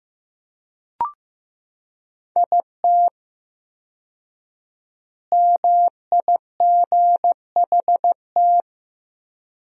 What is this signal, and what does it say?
Morse code, heard as a keyed tone, about 15 wpm, 700 Hz. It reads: IT MIGHT
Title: 4x Word-Spacing - Sentences from Top 300 Words - 15wpm